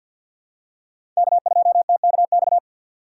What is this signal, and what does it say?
Morse code, heard as a keyed tone, about 51 wpm, 700 Hz. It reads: K2TKX
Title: USA Call Signs - 5 Characters - 3x Character Spacing - 50wpm